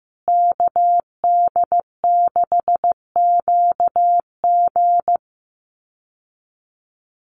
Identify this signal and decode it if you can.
Morse code, heard as a keyed tone, about 15 wpm, 700 Hz. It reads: KD6QG
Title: USA Call Signs Repeated 3x 15wpm